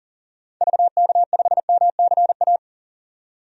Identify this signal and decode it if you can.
Morse code, heard as a keyed tone, about 40 wpm, 700 Hz. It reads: VK5MCA
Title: DXCC-100 Call Signs - 6 Characters - 40wpm